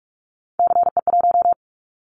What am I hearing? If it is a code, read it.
Morse code, heard as a keyed tone, about 44 wpm, 700 Hz. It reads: CE1
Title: Callsign Prefixes for Top 100 DXCC - 45wpm